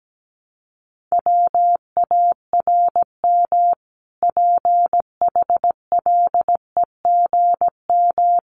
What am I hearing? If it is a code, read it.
Morse code, heard as a keyed tone, about 17 wpm, 700 Hz. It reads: WARM PHLEGM
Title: A Sound Of Thunder 17wpm